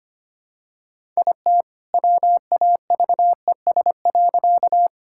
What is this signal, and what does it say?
Morse code, heard as a keyed tone, about 25 wpm, 700 Hz. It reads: IT WAVES.
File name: practice / 25wpm / All-Age Appropriate Jokes - 25wpm